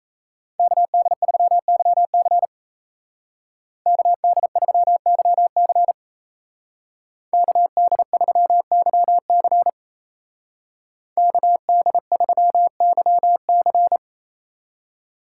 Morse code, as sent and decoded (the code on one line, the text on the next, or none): -.- -.. ...-- -.-- -.-. / -.- -.. ...-- -.-- -.-. / -.- -.. ...-- -.-- -.-. / -.- -.. ...-- -.-- -.-.
KD3YC KD3YC KD3YC KD3YC